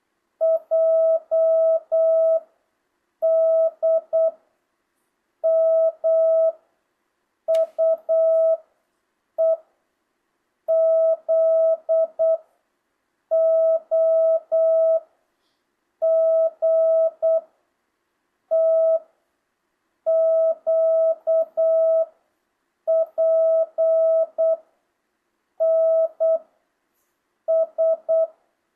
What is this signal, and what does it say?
Morse code, heard as a keyed tone, about 8 wpm, 660 Hz. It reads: J D M U E Z O G T Q P N S
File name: jh8cbh / 2025cw